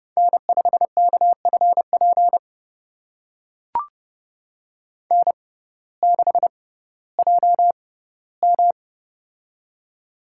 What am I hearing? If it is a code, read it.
Morse code, heard as a keyed tone, about 30 wpm, spaced as for 17 wpm, 700 Hz. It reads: N5KFP N6JM
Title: USA Call Signs - 6x Character Spacing - 30wpm